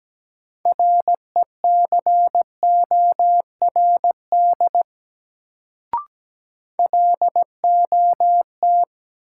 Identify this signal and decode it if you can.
Morse code, heard as a keyed tone, about 17 wpm, 700 Hz. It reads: RECORD LOT
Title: Sentences from Top 500 Words - New Word Drill - 17wpm